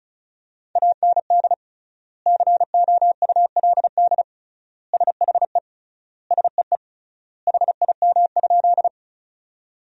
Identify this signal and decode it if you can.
Morse code, heard as a keyed tone, about 34 wpm, 700 Hz. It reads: AND COULD SHE SEE HIM?
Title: 3x Word-Spacing - Sentences from Top 100 Words - 35wpm